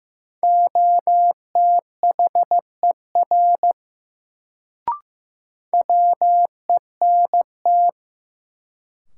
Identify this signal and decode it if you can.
Morse code, heard as a keyed tone, about 15 wpm, 700 Hz. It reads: OTHER WENT
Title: Sets of 2 Words - New Word Drill - 15wpm